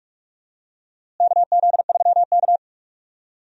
Morse code, heard as a keyed tone, about 46 wpm, 700 Hz. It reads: KZ3K
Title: USA Call Signs - 4 Characters - Repeated 3x - 45wpm